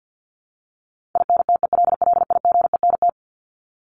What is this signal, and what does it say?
Morse code, heard as a keyed tone, about 50 wpm, 700 Hz. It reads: INTELLIGENT
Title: Top 751-1000 Words 50wpm